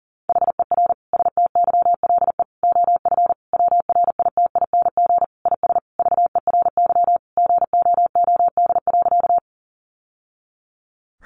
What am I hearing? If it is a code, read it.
Morse code, heard as a keyed tone, about 40 wpm, 700 Hz. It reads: HER STYLE OF WRITING IS VERY GOOD.
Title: Sentences from Top 800 Words 40wpm